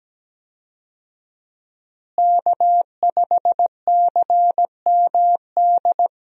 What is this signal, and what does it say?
Morse code, heard as a keyed tone, about 17 wpm, 700 Hz. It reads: K5CMD
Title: USA Call Signs - 5 Characters - Repeated 2x - 17wpm